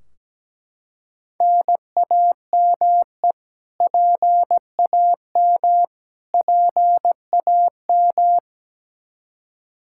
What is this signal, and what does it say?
Morse code, heard as a keyed tone, about 17 wpm, 700 Hz. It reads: NAME PAM PAM
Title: Practice QSO - Exchange 1 - Beginning - 17wpm